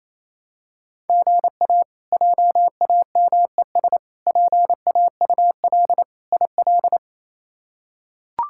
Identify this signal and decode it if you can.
Morse code, heard as a keyed tone, about 28 wpm, 700 Hz. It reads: GA JAMES PAUL IL
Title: K1USN SST QSO Practice - 28wpm